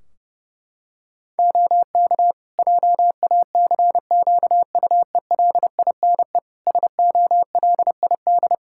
Morse code, heard as a keyed tone, about 30 wpm, 700 Hz. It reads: OK JACQUELINE SOLID
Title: K1USN SST QSO Practice 30wpm